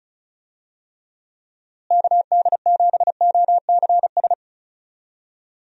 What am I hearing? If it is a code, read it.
Morse code, heard as a keyed tone, about 34 wpm, 700 Hz. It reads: KD7OCS
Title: USA Call Signs - 6 Characters - Repeated 3x - 35wpm